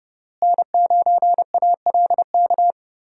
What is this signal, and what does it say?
Morse code, heard as a keyed tone, about 32 wpm, 700 Hz. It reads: N9ALK
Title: USA Call Signs - 5 Characters - Repeated 3x - 30wpm